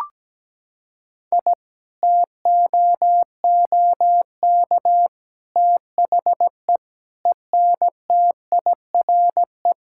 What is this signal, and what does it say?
Morse code, heard as a keyed tone, about 17 wpm, 700 Hz. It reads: I TOOK THE ENTIRE
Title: Sentences from Top 1000 Words - Review Part 5 - 17wpm